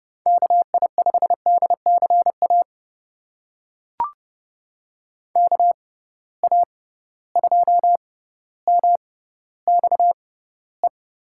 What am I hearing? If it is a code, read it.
Morse code, heard as a keyed tone, about 30 wpm, spaced as for 15 wpm, 700 Hz. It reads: KI5DCA KA2MXE
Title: USA Call Signs - 6 Characters - 6x Character Spacing - 30wpm